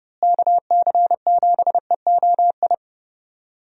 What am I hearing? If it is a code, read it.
Morse code, heard as a keyed tone, about 30 wpm, 700 Hz. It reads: KC7EOI
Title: USA Call Signs - 6 Characters - Repeated 2x - 30wpm